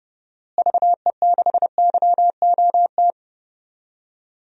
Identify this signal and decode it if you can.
Morse code, heard as a keyed tone, about 32 wpm, 700 Hz. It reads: VE6YOT
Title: DXCC-100 Call Signs - 6 Characters - Repeated 3x - 30wpm